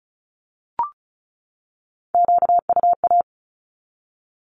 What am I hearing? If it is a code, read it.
Morse code, heard as a keyed tone, about 34 wpm, 700 Hz. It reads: QUA
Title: Q-Codes 35wpm